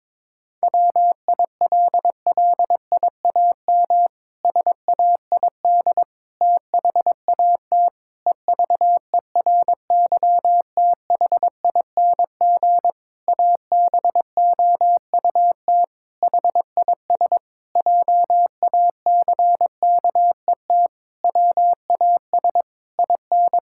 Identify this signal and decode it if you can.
Morse code, heard as a keyed tone, about 22 wpm, 700 Hz. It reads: WILLIAM SAID THAT EVERYTHING ABOUT HIS JACKET WAS IN
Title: Pangram Challenge v2 - 22wpm